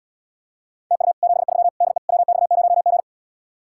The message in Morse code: TU 73 DE WX0G

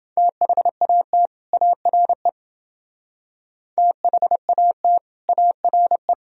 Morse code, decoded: THAT ARE THAT ARE

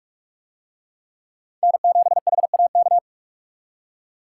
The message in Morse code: N7HAK

